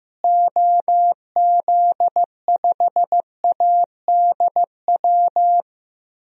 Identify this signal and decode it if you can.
Morse code, heard as a keyed tone, about 15 wpm, 700 Hz. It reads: OZ5ADW
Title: DXCC-100 Call Signs - 6 Characters - Repeated 3x - 15wpm